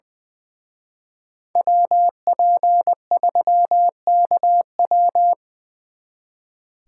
Morse code, heard as a keyed tone, about 25 wpm, 700 Hz. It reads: WP3KW
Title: Speed Racing - USA 2x2 Call Signs 20wpm